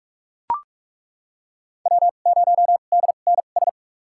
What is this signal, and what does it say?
Morse code, heard as a keyed tone, about 42 wpm, spaced as for 31 wpm, 700 Hz. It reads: W0DNS
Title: USA Call Signs - 5 Characters - 2x Character Spacing - 45wpm